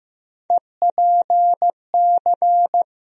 Morse code, heard as a keyed tone, about 16 wpm, 700 Hz. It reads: EPC
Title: USA Call Sign Postfixes - 3 Letters 15wpm